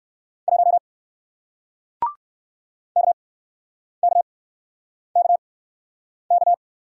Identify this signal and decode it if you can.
Morse code, heard as a keyed tone, about 55 wpm, spaced as for 9 wpm, 700 Hz. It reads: 1 KKKK
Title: Speed-Racing Single Letter-Number 45wpm